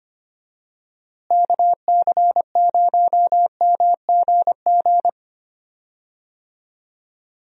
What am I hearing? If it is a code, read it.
Morse code, heard as a keyed tone, about 25 wpm, 700 Hz. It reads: KC0MGG
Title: USA Call Signs - 6 Characters - Repeated 2x - 25wpm